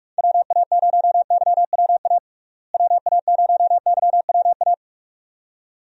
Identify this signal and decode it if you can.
Morse code, heard as a keyed tone, about 44 wpm, 700 Hz. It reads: WA0YWA WA0YWA
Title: USA Call Signs - 6 Characters - Repeated 2x - 45wpm